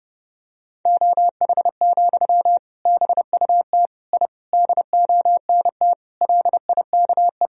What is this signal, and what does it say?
Morse code, heard as a keyed tone, about 32 wpm, 700 Hz. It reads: OH, BUT I DONT LIKE
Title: Sentences from Top 1000 Words - Review Part 3 - 30wpm